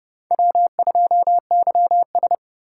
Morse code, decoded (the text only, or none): W2YS